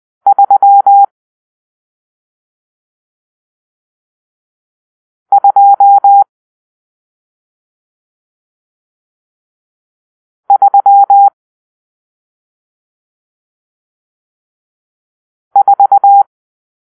Morse code, heard as a keyed tone, about 20 wpm, 800 Hz. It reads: 3 2 3 4
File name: arrl / Grp-6 Drill-1